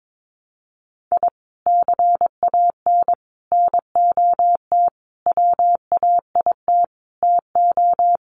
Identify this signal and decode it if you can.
Morse code, heard as a keyed tone, about 22 wpm, 700 Hz. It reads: I CAN NOT WAIT TO
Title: Sets of 5 Words Encore 22wpm